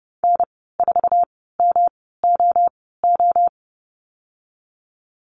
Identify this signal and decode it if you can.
Morse code, heard as a keyed tone, about 32 wpm, spaced as for 17 wpm, 700 Hz. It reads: N4MOO
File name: course / 30:10wpm / 116 Lesson 19g - Letter w - Call Signs 30wpm